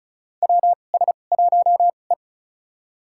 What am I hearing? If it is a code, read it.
Morse code, heard as a keyed tone, about 36 wpm, spaced as for 25 wpm, 700 Hz. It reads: WS1E
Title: USA Call Signs - 4 Characters - 2x Character Spacing - 35wpm